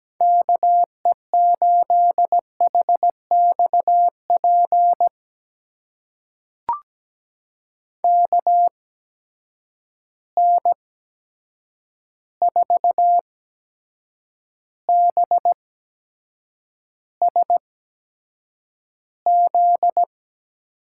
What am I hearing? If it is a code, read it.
Morse code, heard as a keyed tone, about 17 wpm, spaced as for 7 wpm, 700 Hz. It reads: KE8HXP KN4BSZ
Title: USA Call Signs - 6 Characters - 8x Character Spacing - 17wpm